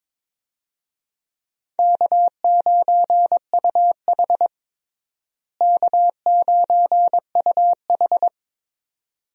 Morse code, heard as a keyed tone, about 22 wpm, 700 Hz. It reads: K9UH K9UH
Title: USA Call Signs - 4 Characters - Repeated 2x - 22wpm